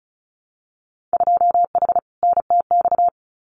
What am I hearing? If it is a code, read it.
Morse code, heard as a keyed tone, about 36 wpm, 700 Hz. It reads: 2H NTX
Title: Winter Field Day Exchanges Encore 35wpm